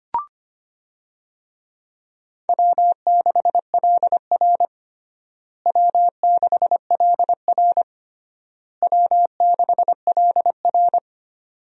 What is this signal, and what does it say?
Morse code, heard as a keyed tone, about 25 wpm, 700 Hz. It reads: W6LR W6LR W6LR
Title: USA Call Signs - 4 Characters - Repeated 3x - 25wpm